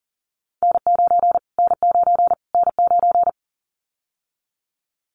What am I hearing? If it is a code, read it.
Morse code, heard as a keyed tone, about 40 wpm, 700 Hz. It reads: N9 N9 N9
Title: Callsign Prefixes for Top 25 DXCC - Repeated 3x - 40wpm